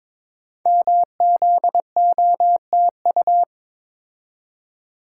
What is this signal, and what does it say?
Morse code, heard as a keyed tone, about 22 wpm, 700 Hz. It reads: MZOTU
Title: Code Groups - 5 Letters - 22wpm